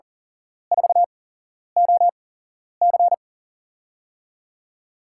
Morse code, heard as a keyed tone, about 42 wpm, spaced as for 11 wpm, 700 Hz. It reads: 4OC
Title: USA Call Signs - 8x Character Spacing - 40wpm